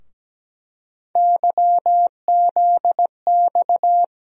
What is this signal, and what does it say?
Morse code, heard as a keyed tone, about 17 wpm, 700 Hz. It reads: YZX
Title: USA Call Sign Postfixes - 3 Letters 17wpm